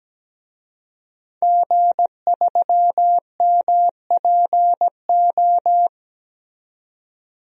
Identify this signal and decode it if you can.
Morse code, heard as a keyed tone, about 17 wpm, 700 Hz. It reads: G3MPO
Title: DXCC-100 Call Signs - 5 Characters - 17wpm